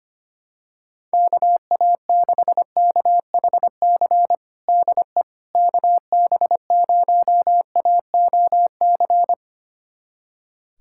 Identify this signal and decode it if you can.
Morse code, heard as a keyed tone, about 25 wpm, 700 Hz. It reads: KA6KHC DE KB0AOC
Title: USA Callsign DE Callsign - 6 Characters - 25wpm